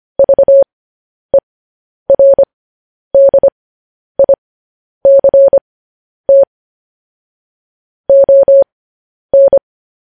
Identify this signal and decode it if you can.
Morse code, heard as a keyed tone, about 25 wpm, spaced as for 10 wpm, 550 Hz. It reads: VERDICT ON
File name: podcast / News headlines-1x-25x10